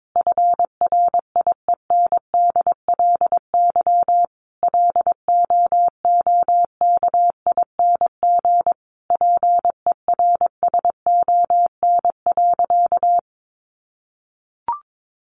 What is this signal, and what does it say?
Morse code, heard as a keyed tone, about 22 wpm, 700 Hz. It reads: FRIENDLY LOOKING PERSON.
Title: Easy Phrases 22wpm